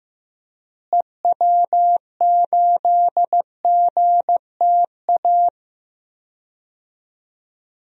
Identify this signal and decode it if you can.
Morse code, heard as a keyed tone, about 15 wpm, 700 Hz. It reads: EW8GTA
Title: DXCC-100 Call Signs - 6 Characters - Repeated 3x - 15wpm